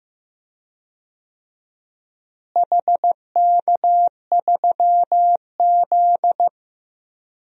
Morse code, heard as a keyed tone, about 15 wpm, 700 Hz. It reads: HK3Z
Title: DXCC-100 Call Signs - 4 Characters - Repeated 3x - 15wpm